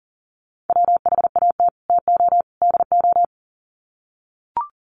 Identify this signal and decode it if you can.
Morse code, heard as a keyed tone, about 40 wpm, 700 Hz. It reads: WHAT TO DO
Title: Sets of 3 Words 40wpm